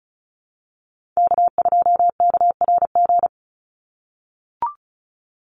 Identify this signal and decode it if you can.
Morse code, heard as a keyed tone, about 34 wpm, 700 Hz. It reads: K2KRG
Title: Call Signs - 35wpm